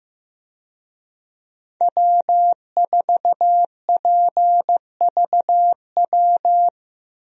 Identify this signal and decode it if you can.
Morse code, heard as a keyed tone, about 15 wpm, 700 Hz. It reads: W4PVW
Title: USA Call Signs - 5 Characters - Repeated 3x - 15wpm